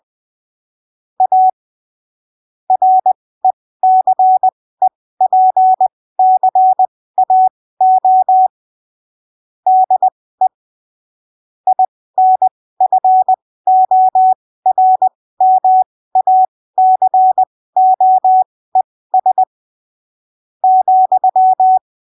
Morse code, heard as a keyed tone, about 20 wpm, spaced as for 15 wpm, 750 Hz. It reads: A RECEPCAO DE INFORMACOES ,
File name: arquivos / texto / texto 18 20ppm